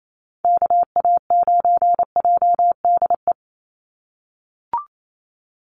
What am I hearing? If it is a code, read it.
Morse code, heard as a keyed tone, about 28 wpm, 700 Hz. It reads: KA9JDE
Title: USA Call Signs 28wpm